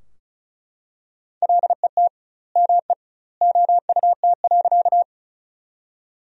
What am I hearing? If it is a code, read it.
Morse code, heard as a keyed tone, about 34 wpm, 700 Hz. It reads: LET ME OUT.
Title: 2x Word-Spacing - Sentences from Top 300 Words - 35wpm